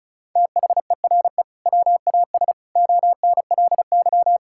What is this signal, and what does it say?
Morse code, heard as a keyed tone, about 34 wpm, 700 Hz. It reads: THERE WAS ONLY
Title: Sentences from Top 1600 Words 35wpm